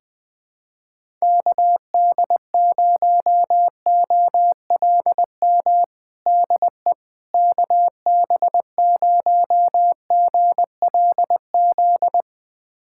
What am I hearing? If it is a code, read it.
Morse code, heard as a keyed tone, about 21 wpm, 700 Hz. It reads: KD0OLM DE KB0GLZ
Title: USA Callsign DE Callsign - 6 Characters - 20wpm